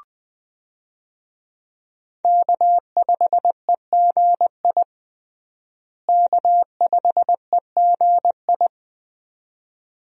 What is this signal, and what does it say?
Morse code, heard as a keyed tone, about 20 wpm, 700 Hz. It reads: K5EGI K5EGI
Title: USA Call Signs - 5 Characters - Repeated 2x - 20wpm